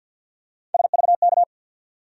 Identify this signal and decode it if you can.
Morse code, heard as a keyed tone, about 48 wpm, 700 Hz. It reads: SVK
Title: USA Call Sign Postfixes - 3 Letters 50wpm